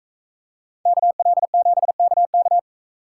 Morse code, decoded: KL7KK

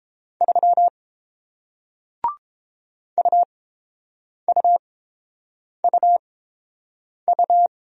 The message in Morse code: 3 UUUU

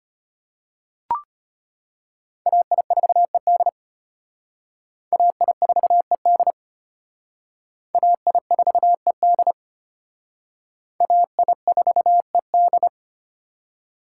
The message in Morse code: AI4ED AI4ED AI4ED AI4ED

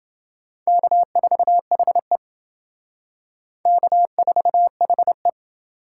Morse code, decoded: K4HE K4HE

